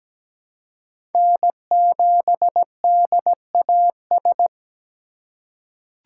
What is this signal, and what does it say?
Morse code, heard as a keyed tone, about 17 wpm, 700 Hz. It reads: N7DAS
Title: USA Call Signs - 2x Character Spacing - 17wpm